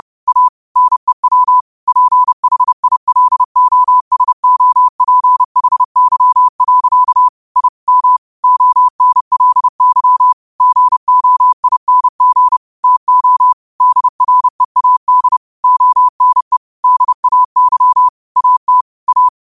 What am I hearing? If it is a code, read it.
Morse code, heard as a keyed tone, about 30 wpm, 1000 Hz. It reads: A NEW PHILOSOPHY. I M ONLY GOING TO DREAD ONE DAY AT A
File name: qotdfiles / 30wpm